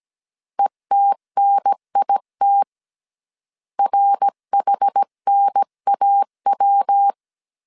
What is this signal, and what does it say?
Morse code, heard as a keyed tone, about 17 wpm, 800 Hz. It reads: ETNIT RHNAW
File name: CW / DET / 1 3 1500C S A 16wpm 2141